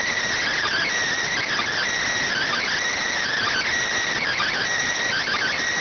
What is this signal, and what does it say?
Emergency Alert System (SAME) tones (header bursts), heard as harsh buzzing bursts. It is heard from 0.4 to 4.6 s.